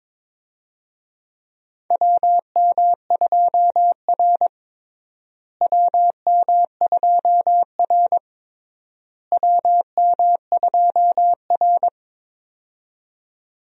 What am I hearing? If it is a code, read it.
Morse code, heard as a keyed tone, about 21 wpm, 700 Hz. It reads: WM2R WM2R WM2R
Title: USA Call Signs - 4 Characters - Repeated 3x - 22wpm